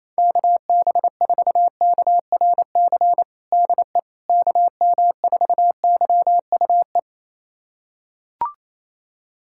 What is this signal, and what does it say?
Morse code, heard as a keyed tone, about 28 wpm, 700 Hz. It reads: KB4KRC DE KM4YUE
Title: USA Callsign DE Callsign - 6 Characters - 28wpm